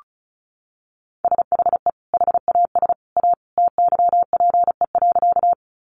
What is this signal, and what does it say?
Morse code, heard as a keyed tone, about 34 wpm, 700 Hz. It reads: SHE HAS A TYPE.
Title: Sentences from Top 600 Words 35wpm